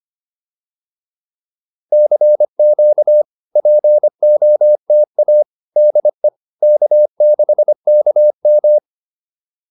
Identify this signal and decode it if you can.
Morse code, heard as a keyed tone, about 25 wpm, 600 Hz. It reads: CQ POTA DE K6KM